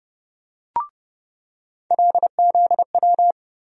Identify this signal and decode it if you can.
Morse code, heard as a keyed tone, about 30 wpm, 700 Hz. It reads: LZW
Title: USA Call Sign Postfixes - 3 Letters 30wpm